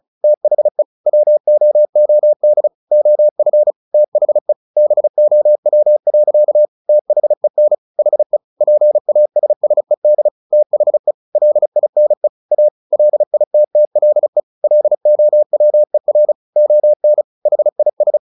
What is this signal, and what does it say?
Morse code, heard as a keyed tone, about 34 wpm, 600 Hz. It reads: THE WOOD OF THE BOW. THEN HE PASSED THE LINE A LITTLE LOWER ON HIS
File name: the old man and the sea 2 35wpm